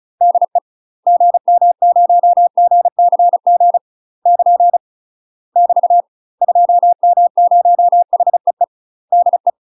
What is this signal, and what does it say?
Morse code, heard as a keyed tone, about 36 wpm, 700 Hz. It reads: DE GM0GCG <KN> <BT> 2M0HEE DE
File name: UK MASTER QSOs TRAINING / 35WPM / UK MASTER QSOs 0033